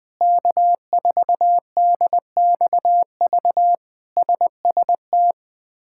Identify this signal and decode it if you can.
Morse code, heard as a keyed tone, about 20 wpm, 700 Hz. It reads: K4DXV SST
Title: K1USN SST QSO Practice - 20wpm